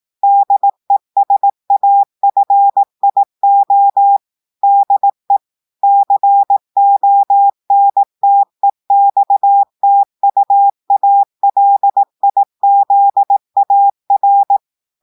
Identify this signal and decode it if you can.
Morse code, heard as a keyed tone, about 18 wpm, 800 Hz. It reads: DESAFIO DE CONTEXTUALIZAR
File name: audios / portugais / portugais texte en clair 1 vitesse 18 mots minutes